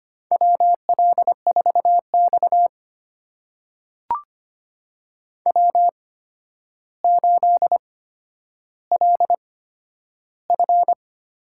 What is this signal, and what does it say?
Morse code, heard as a keyed tone, about 25 wpm, spaced as for 11 wpm, 700 Hz. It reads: WL4X W8LF
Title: USA Call Signs - 4 Characters - 8x Character Spacing - 25wpm